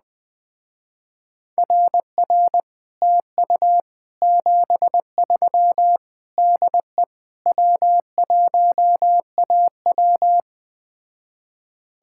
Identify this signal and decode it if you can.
Morse code, heard as a keyed tone, about 20 wpm, 700 Hz. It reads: RR TU 73 DE W1AW
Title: Speed-Racing Common QSO Phrases - 17wpm